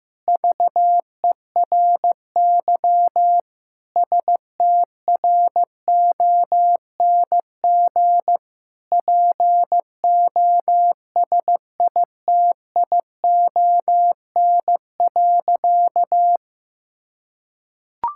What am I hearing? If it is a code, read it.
Morse code, heard as a keyed tone, about 15 wpm, 700 Hz. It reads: VERY STRONG POSITION.
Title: Sentences from Top 1000 Words - Review Part 5 - 15wpm